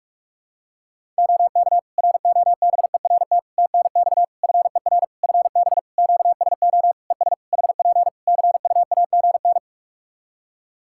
Morse code, heard as a keyed tone, about 46 wpm, 700 Hz. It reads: OK ROBERT TNX FER FB QSO ES HP CUAGN
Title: K1USN SST QSO Practice 45wpm